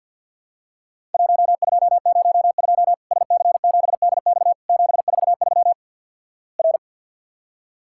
Morse code, heard as a keyed tone, about 50 wpm, 700 Hz. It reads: 1202 SC7DX 743 R
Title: Worked All Europe DX Contest - QTC Exchange - 50wpm